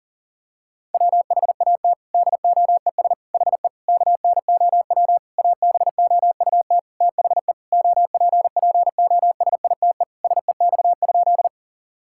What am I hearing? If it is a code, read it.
Morse code, heard as a keyed tone, about 40 wpm, 700 Hz. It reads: WHAT DOES HE KNOW ABOUT THE OPPOSITE SEX?
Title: Sentences from Top 1600 Words 40wpm